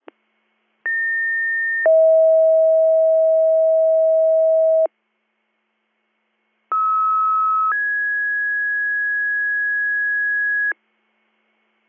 Two-tone page, heard as a one-second tone, then a three-second tone, about 1800 then 640 Hz, about 1290 then 1650 Hz. It is heard from 0.9 to 10.7 s.